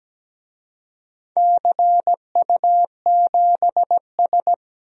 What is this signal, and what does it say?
Morse code, heard as a keyed tone, about 17 wpm, 700 Hz. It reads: CU7S
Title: DXCC-100 Call Signs - 4 Characters - Repeated 2x - 17wpm